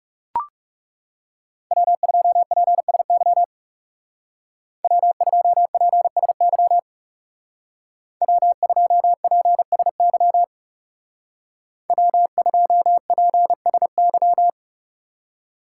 Morse code, heard as a keyed tone, about 36 wpm, 700 Hz. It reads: W2PSY W2PSY W2PSY W2PSY
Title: Speed Racing - USA 1x3 Call Signs 30wpm